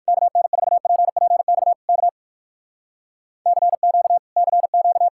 Morse code, heard as a keyed tone, about 55 wpm, 700 Hz. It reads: KN4PPX K CQ CQ
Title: Speed-Racing Common QSO Phrases - 45wpm